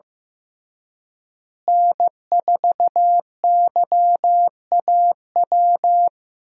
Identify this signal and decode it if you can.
Morse code, heard as a keyed tone, about 15 wpm, 700 Hz. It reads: N4YAW
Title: USA Call Signs - 5 Characters - Repeated 3x - 15wpm